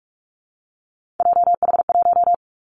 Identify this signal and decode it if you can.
Morse code, heard as a keyed tone, about 44 wpm, 700 Hz. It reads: JH1
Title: Callsign Prefixes for Top 25 DXCC - Repeated 3x - 45wpm